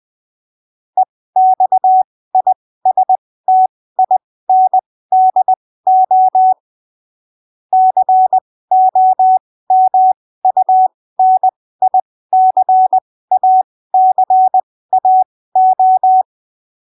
Morse code, heard as a keyed tone, about 20 wpm, spaced as for 15 wpm, 750 Hz. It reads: EXISTINDO COMUNICACAO
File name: arquivos / texto / texto 04 20ppm